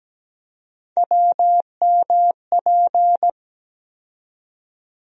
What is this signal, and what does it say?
Morse code, heard as a keyed tone, about 17 wpm, 700 Hz. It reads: WMP
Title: USA Call Sign Postfixes - 3 Letters 17wpm